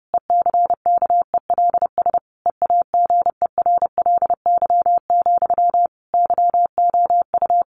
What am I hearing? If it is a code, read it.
Morse code, heard as a keyed tone, about 30 wpm, 700 Hz. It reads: ECKELS EAGERLY, YOU
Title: A Sound Of Thunder 30wpm